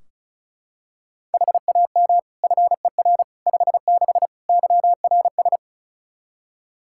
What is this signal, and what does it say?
Morse code, heard as a keyed tone, about 34 wpm, 700 Hz. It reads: HAM FER 56 YRS
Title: Practice QSO - Exchange 3 - More Info - 35wpm